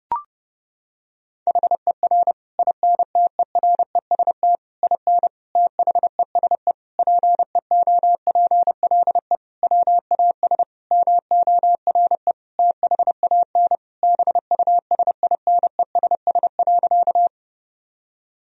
Morse code, heard as a keyed tone, about 30 wpm, 700 Hz. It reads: HER INTEREST IN THESE PEOPLE WAS MORE THAN BUSINESS.
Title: Sentences from Top 1000 Words - Review Part 2 - 30wpm